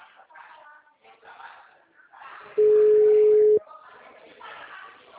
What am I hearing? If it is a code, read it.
Ringback tone (Europe, 425 Hz).
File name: out- -2014- - - .72973